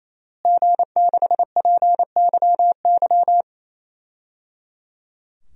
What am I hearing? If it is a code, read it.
Morse code, heard as a keyed tone, about 27 wpm, 700 Hz. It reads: G6PYY
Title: DXCC-100 Call Signs - 5 Characters - Repeated 3x - 28wpm